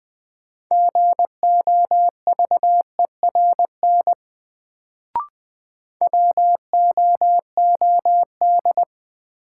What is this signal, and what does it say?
Morse code, heard as a keyed tone, about 20 wpm, 700 Hz. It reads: GOVERN WOOD
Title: Sentences from Top 1000 Words - Review Part 2 - New Word Drill - 20wpm